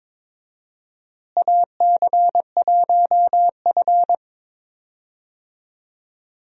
Morse code, heard as a keyed tone, about 23 wpm, 700 Hz. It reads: AC1F
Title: USA Call Signs - 4 Characters - Repeated 2x - 22wpm